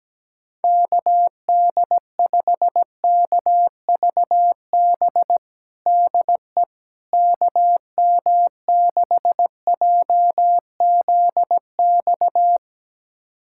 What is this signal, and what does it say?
Morse code, heard as a keyed tone, about 17 wpm, 700 Hz. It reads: KD5KVB DE KM6JZX
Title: USA Callsign DE Callsign - 6 Characters - 17wpm